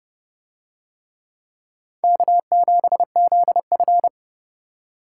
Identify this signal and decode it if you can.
Morse code, heard as a keyed tone, about 29 wpm, 700 Hz. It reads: K7ZF